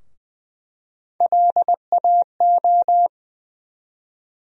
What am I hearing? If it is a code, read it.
Morse code, heard as a keyed tone, about 20 wpm, 700 Hz. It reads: LAO
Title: USA Call Sign Postfixes - 3 Letters 20wpm